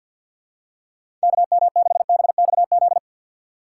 Morse code, heard as a keyed tone, about 52 wpm, 700 Hz. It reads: KM6BXZ
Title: USA Call Signs - 6 Characters - 3x Character Spacing - 50wpm